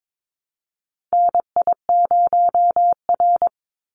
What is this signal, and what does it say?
Morse code, heard as a keyed tone, about 22 wpm, 700 Hz. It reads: NI0R
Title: Call Signs - 22wpm